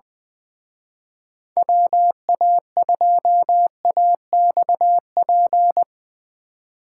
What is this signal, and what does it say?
Morse code, heard as a keyed tone, about 22 wpm, 700 Hz. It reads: WA2AXP
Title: Speed Racing - USA Call Signs 20wpm